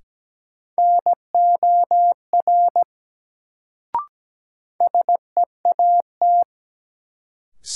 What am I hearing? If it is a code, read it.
Morse code, heard as a keyed tone, about 17 wpm, 700 Hz. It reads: NOR SEAT
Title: Sentences from Top 1500 Words - New Word Drill Alt - 17wpm